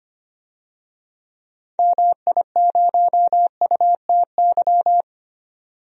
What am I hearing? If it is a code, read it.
Morse code, heard as a keyed tone, about 25 wpm, 690 Hz. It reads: MI0UTY
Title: DXCC-100 Call Signs - 6 Characters - 25wpm